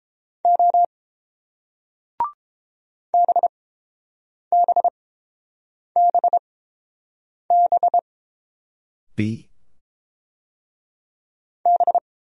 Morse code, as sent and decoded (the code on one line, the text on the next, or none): --- / -... -... -... -... / -...
O BBBB B